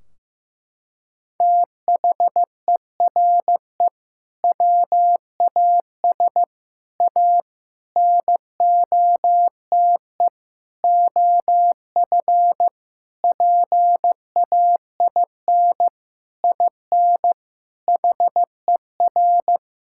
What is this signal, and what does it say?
Morse code, heard as a keyed tone, about 15 wpm, 700 Hz. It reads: THERE WAS A NOTE OF PAIN IN HER
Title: Sentences from Top 1600 Words 15wpm